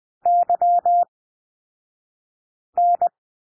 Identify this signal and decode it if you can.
Morse code, heard as a keyed tone, about 20 wpm, spaced as for 5 wpm, 700 Hz. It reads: YN
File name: SOUND LETTERS 20 wpm